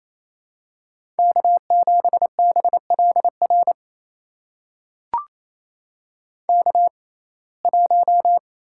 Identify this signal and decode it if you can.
Morse code, heard as a keyed tone, about 28 wpm, spaced as for 20 wpm, 700 Hz. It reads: K7BLR K1
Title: USA Call Signs - 5 Characters - 6x Character Spacing - 28wpm